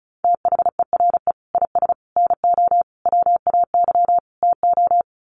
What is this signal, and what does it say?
Morse code, heard as a keyed tone, about 34 wpm, 700 Hz. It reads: THERE IS NO WAY TO
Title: Sets of 5 Words Encore 35wpm